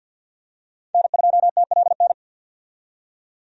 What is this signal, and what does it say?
Morse code, heard as a keyed tone, about 53 wpm, 700 Hz. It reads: N2TLN
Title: USA Call Signs - 5 Characters - 50wpm